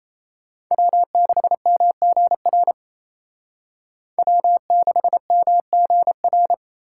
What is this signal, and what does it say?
Morse code, heard as a keyed tone, about 36 wpm, 700 Hz. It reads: W6MGR W6MGR